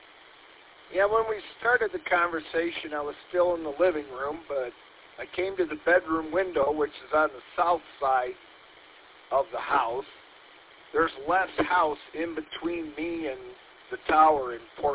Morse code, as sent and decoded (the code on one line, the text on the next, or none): none